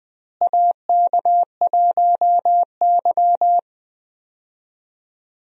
Morse code, heard as a keyed tone, about 21 wpm, 700 Hz. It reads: AK1Y
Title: USA Call Signs - 4 Characters - Repeated 3x - 20wpm